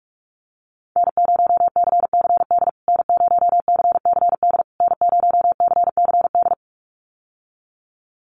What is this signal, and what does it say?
Morse code, heard as a keyed tone, about 46 wpm, 700 Hz. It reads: N0CCD N0CCD N0CCD
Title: USA Call Signs Repeated 3x 45wpm